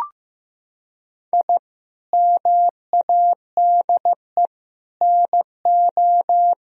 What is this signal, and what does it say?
Morse code, heard as a keyed tone, about 15 wpm, 700 Hz. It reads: I MADE NO
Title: Sentences from Top 1000 Words - Review Part 1 - 15wpm